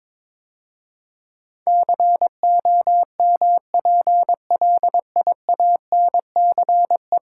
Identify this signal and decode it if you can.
Morse code, heard as a keyed tone, about 22 wpm, 700 Hz. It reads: COMPLIANCE
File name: practice / 22wpm / 10 Letter Words - First 3 Letters Given - 22wpm